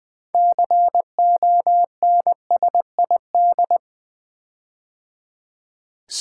Morse code, heard as a keyed tone, about 20 wpm, 700 Hz. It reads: CONSID
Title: Build A Word - 6 Letters At A Time - 12-Letter Words - 20wpm